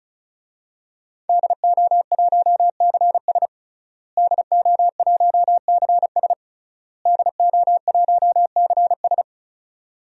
Morse code, heard as a keyed tone, about 36 wpm, 700 Hz. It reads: DO1CS DO1CS DO1CS
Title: DXCC-100 Call Signs - 5 Characters - Repeated 3x - 35wpm